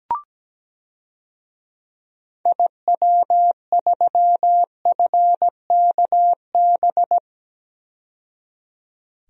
Morse code, heard as a keyed tone, about 17 wpm, 700 Hz. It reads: IW3FKB
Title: DXCC-100 Call Signs - 6 Characters - 17wpm